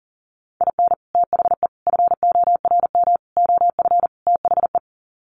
Morse code, heard as a keyed tone, about 37 wpm, 700 Hz. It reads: IN THE FORM OF THE
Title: Sets of 5 Words Encore2 40wpm